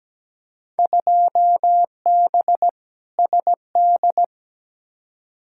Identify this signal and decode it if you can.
Morse code, heard as a keyed tone, about 17 wpm, 700 Hz. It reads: 2B SD
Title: Field Day Exchanges Encore - 17wpm